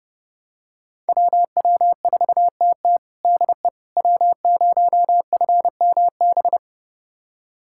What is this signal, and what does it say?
Morse code, heard as a keyed tone, about 30 wpm, 700 Hz. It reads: WW4TT DE W0FMB
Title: USA Callsign DE Callsign - 5 Characters - 30wpm